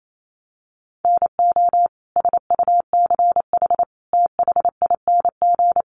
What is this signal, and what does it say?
Morse code, heard as a keyed tone, about 28 wpm, 700 Hz. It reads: NO SUCH THING
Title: Sets of 5 Words Encore 28wpm